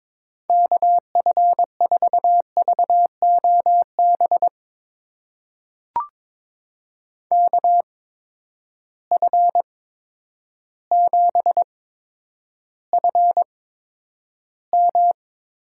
Morse code, heard as a keyed tone, about 22 wpm, spaced as for 10 wpm, 700 Hz. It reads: KF4VOB KF7FM
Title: USA Call Signs - 6 Characters - 8x Character Spacing - 22wpm